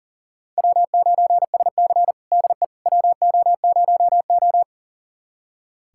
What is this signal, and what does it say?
Morse code, heard as a keyed tone, about 40 wpm, 700 Hz. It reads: W9SC DE WO0O
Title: USA Callsign DE Callsign - 4 Characters - 40wpm